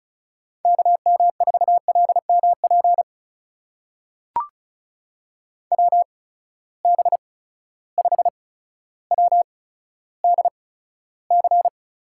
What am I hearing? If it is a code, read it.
Morse code, heard as a keyed tone, about 36 wpm, spaced as for 15 wpm, 700 Hz. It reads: KM4LMP WB5WDC
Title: USA Call Signs - 8x Character Spacing - 35wpm